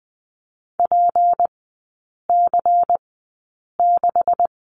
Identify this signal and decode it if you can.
Morse code, heard as a keyed tone, about 20 wpm, spaced as for 8 wpm, 700 Hz. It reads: PC6
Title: 178 Lesson 29e - Slash - Three Characters 20wpm